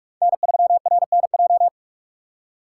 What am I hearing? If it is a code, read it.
Morse code, heard as a keyed tone, about 46 wpm, 700 Hz. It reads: N3RNJ
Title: USA Call Signs - 5 Characters - 4x Character Spacing - 45wpm